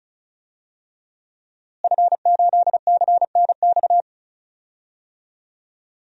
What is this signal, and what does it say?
Morse code, heard as a keyed tone, about 34 wpm, 700 Hz. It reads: F8CNX
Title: DXCC-100 Call Signs - 5 Characters - Repeated 2x - 35wpm